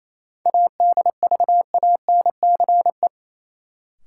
Morse code, heard as a keyed tone, about 28 wpm, 700 Hz. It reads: ADVANCE
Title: Sentences from Top 1400 Words - New Word Drill - 28wpm